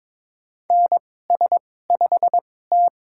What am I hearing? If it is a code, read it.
Morse code, heard as a keyed tone, about 22 wpm, spaced as for 16 wpm, 700 Hz. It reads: NS5T
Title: USA Call Signs - 2x Character Spacing - 22wpm